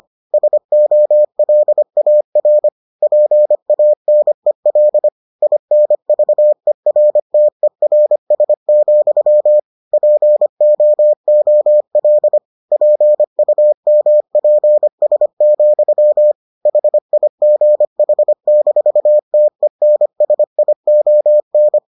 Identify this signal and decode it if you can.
Morse code, heard as a keyed tone, about 25 wpm, 600 Hz. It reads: SOLAR PANEL INVERTERS, POOL PUMPS, HIGH-TENSION